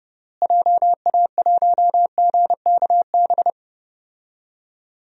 Morse code, heard as a keyed tone, about 32 wpm, 700 Hz. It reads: JA1GKB
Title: DXCC-100 Call Signs - 6 Characters - Repeated 2x - 30wpm